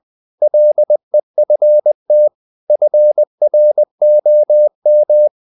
Morse code, heard as a keyed tone, about 20 wpm, 600 Hz. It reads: LEFT FROM
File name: the old man and the sea part 2 20wpm